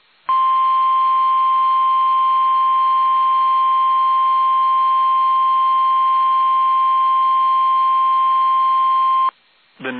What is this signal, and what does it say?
Emergency Alert System (SAME) tones (the 1050 Hz attention tone). It is heard from 0.2 to 9.3 s.
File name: eas / KUSO / KUSO